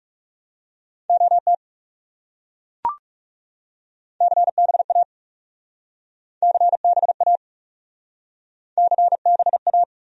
Morse code, heard as a keyed tone, about 38 wpm, 700 Hz. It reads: OT CBA CBA CBA